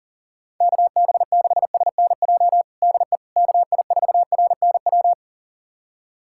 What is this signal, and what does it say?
Morse code, heard as a keyed tone, about 40 wpm, 700 Hz. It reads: KB6SNJ DE KI4RNW
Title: USA Callsign DE Callsign - 6 Characters - 40wpm